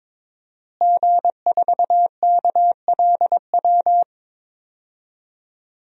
Morse code, heard as a keyed tone, about 21 wpm, 700 Hz. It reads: G4KLW
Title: DXCC-100 Call Signs - 5 Characters - Repeated 3x - 22wpm